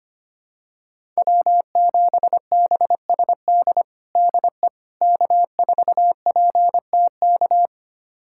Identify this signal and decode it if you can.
Morse code, heard as a keyed tone, about 25 wpm, 700 Hz. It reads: W7BSD DE K4PTK
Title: USA Callsign DE Callsign - 5 Characters - 25wpm